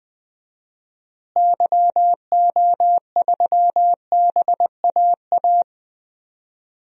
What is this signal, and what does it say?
Morse code, heard as a keyed tone, about 20 wpm, 700 Hz. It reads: YO3BAA
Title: DXCC-100 Call Signs - 6 Characters - Repeated 3x - 20wpm